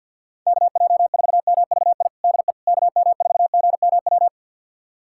Morse code, heard as a keyed tone, about 47 wpm, 700 Hz. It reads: KJ4MVI DE KM4GMW
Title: USA Callsign DE Callsign - 6 Characters - 50wpm